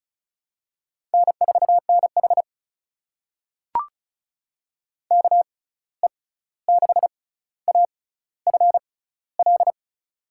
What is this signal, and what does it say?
Morse code, heard as a keyed tone, about 36 wpm, spaced as for 16 wpm, 700 Hz. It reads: N4NH KE6AFL
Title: USA Call Signs - 6x Character Spacing - 35wpm